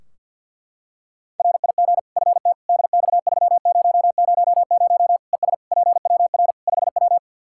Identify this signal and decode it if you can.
Morse code, heard as a keyed tone, about 47 wpm, 700 Hz. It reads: RIG FT DX3000 ES PWR 5W
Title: K1USN SST QSO Practice 50wpm